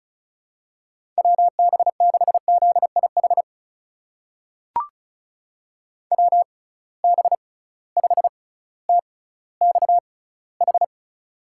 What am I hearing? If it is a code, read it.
Morse code, heard as a keyed tone, about 34 wpm, spaced as for 18 wpm, 700 Hz. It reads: WB6ZIH WB5TXH
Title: USA Call Signs - 6 Characters - 6x Character Spacing - 35wpm